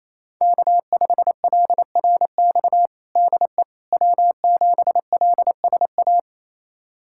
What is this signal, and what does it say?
Morse code, heard as a keyed tone, about 28 wpm, 700 Hz. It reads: K5LRX DE W7LSA
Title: USA Callsign DE Callsign 28wpm